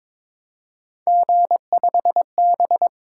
Morse code, heard as a keyed tone, about 22 wpm, 700 Hz. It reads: G5B
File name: practice / 22wpm / Three Letter-Number Combinations - 22wpm